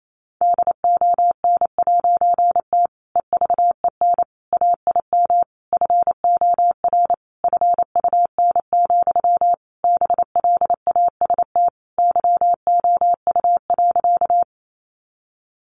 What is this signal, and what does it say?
Morse code, heard as a keyed tone, about 28 wpm, 700 Hz. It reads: DON'T EVEN AIM FOR FUN, BLAST YOU.
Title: A Sound Of Thunder 28wpm